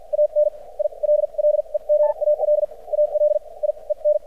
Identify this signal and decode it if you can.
Morse code, heard as a keyed tone, about 50 wpm, spaced as for 37 wpm, 600 Hz, with radio noise. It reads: AN ILLEGAL ALIEN
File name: archived / CRSnet